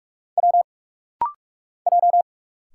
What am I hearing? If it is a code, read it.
Morse code, heard as a keyed tone, about 46 wpm, 690 Hz. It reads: W J
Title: Single Letters - Rapid-Fire 45wpm